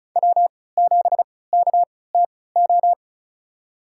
Morse code, heard as a keyed tone, about 34 wpm, spaced as for 20 wpm, 700 Hz. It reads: W7KTO
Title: USA Call Signs - 5 Characters - 3x Character Spacing - 35wpm